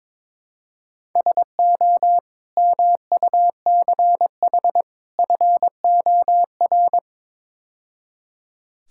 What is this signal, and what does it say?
Morse code, heard as a keyed tone, about 22 wpm, 700 Hz. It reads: SO MUCH FOR
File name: practice / 22wpm / Build A Phrase - 5-Word Phrases - 22wpm